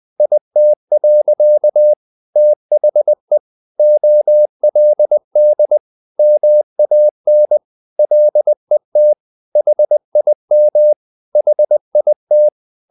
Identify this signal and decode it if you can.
Morse code, heard as a keyed tone, about 20 wpm, 600 Hz. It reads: IT. THE OLD MAN LET HIM HIT
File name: the old man and the sea part 2 20wpm